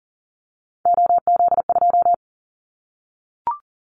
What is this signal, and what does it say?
Morse code, heard as a keyed tone, about 40 wpm, 700 Hz. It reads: OZ2
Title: Callsign Prefixes for Top 100 DXCC - 40wpm